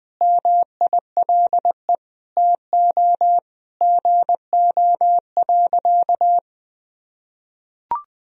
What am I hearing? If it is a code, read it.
Morse code, heard as a keyed tone, about 20 wpm, 700 Hz. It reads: MILE TO GO.
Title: Sentences from Top 1000 Words - Review Part 1 - 20wpm